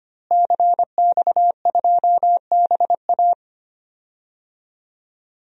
Morse code, heard as a keyed tone, about 25 wpm, 700 Hz. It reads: CX2BA
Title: DXCC-100 Call Signs - 5 Characters - Repeated 2x - 25wpm